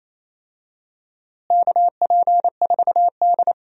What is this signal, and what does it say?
Morse code, heard as a keyed tone, about 28 wpm, 700 Hz. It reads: KP4D
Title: DXCC-100 Call Signs - 4 Characters - Repeated 2x - 28wpm